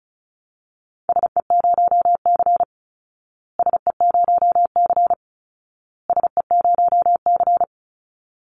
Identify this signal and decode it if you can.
Morse code, heard as a keyed tone, about 34 wpm, 700 Hz. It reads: SE0C SE0C SE0C
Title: Call Signs Repeated 2x - 35wpm